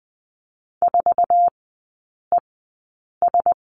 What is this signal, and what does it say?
Morse code, heard as a keyed tone, about 20 wpm, spaced as for 8 wpm, 700 Hz. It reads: 4ES
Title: 152 Lesson 25e - Letter p - Three Characters 20wpm